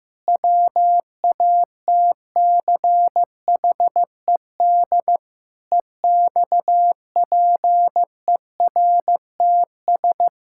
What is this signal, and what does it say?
Morse code, heard as a keyed tone, about 15 wpm, 700 Hz. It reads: WATCHED EXPERTS
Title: Pangram Challenge v2 - 15wpm